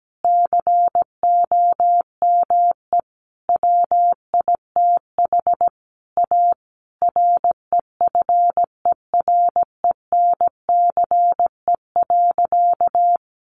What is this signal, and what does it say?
Morse code, heard as a keyed tone, about 17 wpm, 700 Hz. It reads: COME WITH A REFERENCE.
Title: Sentences from Top 1200 Words 17wpm